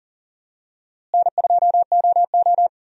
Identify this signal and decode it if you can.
Morse code, heard as a keyed tone, about 40 wpm, 700 Hz. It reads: N2OO
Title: USA Call Signs - 4 Characters - Repeated 3x - 40wpm